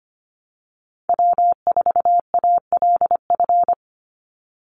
Morse code, heard as a keyed tone, about 25 wpm, 700 Hz. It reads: W4ALF
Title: Call Signs - 25wpm